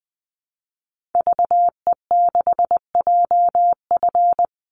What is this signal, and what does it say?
Morse code, heard as a keyed tone, about 20 wpm, 700 Hz. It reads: VE6JF
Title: Call Signs - 20wpm